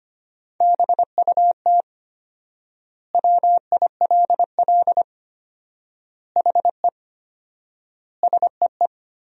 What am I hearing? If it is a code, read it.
Morse code, heard as a keyed tone, about 25 wpm, 700 Hz. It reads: BUT WILL HE SEE
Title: 4x Word-Spacing - Sentences from Top 100 Words - 25wpm